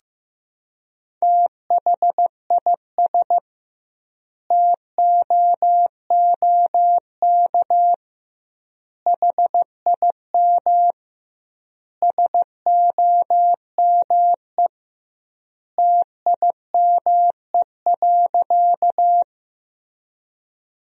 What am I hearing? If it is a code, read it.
Morse code, heard as a keyed tone, about 15 wpm, 700 Hz. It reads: THIS TOOK HIM SOME TIME.
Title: 2x Word-Spacing - Sentences from Top 300 Words - 15wpm